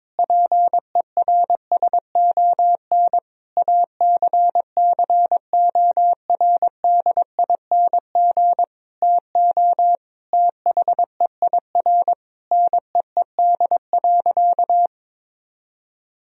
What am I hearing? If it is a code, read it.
Morse code, heard as a keyed tone, about 22 wpm, 700 Hz. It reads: PERSON ACCORDING TO THEIR NEED.
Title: Sentences from Top 1000 Words - Review Part 4 - 22wpm